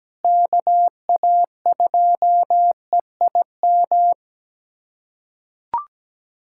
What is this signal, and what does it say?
Morse code, heard as a keyed tone, about 17 wpm, 700 Hz. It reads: KA2EIM
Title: USA Callsign DE Callsign - 6 Characters - 17wpm